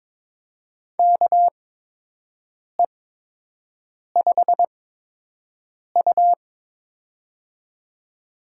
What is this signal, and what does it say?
Morse code, heard as a keyed tone, about 22 wpm, spaced as for 6 wpm, 700 Hz. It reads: KE5U
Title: USA Call Signs - 8x Character Spacing - 22wpm